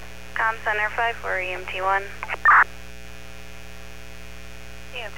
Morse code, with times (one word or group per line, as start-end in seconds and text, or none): none